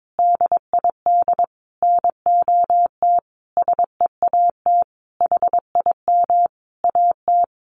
Morse code, heard as a keyed tone, about 22 wpm, 700 Hz. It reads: DID NOT SEAT HIM AT
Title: Sentences from Top 1500 Words 22wpm